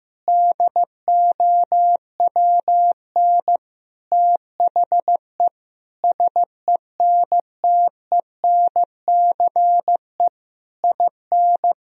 Morse code, heard as a keyed tone, about 15 wpm, 700 Hz. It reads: DOWN THE SENTENCE IN
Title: Sentences from Top 1000 Words - Review Part 1 - 15wpm